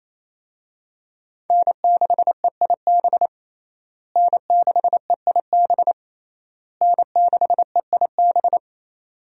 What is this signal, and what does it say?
Morse code, heard as a keyed tone, about 27 wpm, 700 Hz. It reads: N6EIB N6EIB N6EIB
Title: USA Call Signs - 5 Characters - Repeated 3x - 28wpm